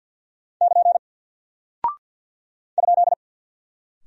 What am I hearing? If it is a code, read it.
Morse code, heard as a keyed tone, about 48 wpm, 700 Hz. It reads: <KN> ?
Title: Single Character (Letter:Number:Punctuation:Prosign) Alt 50wpm